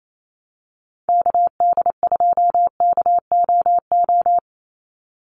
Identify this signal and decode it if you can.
Morse code, heard as a keyed tone, about 28 wpm, 700 Hz. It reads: KD2KOO
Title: USA Call Signs 28wpm